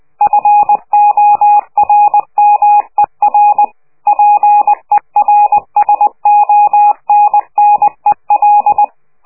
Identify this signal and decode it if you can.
Morse code, heard as a keyed tone, about 20 wpm, 830 Hz. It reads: FORMER PERSONNEL
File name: worldcoastal / scheveningenradio / morsepch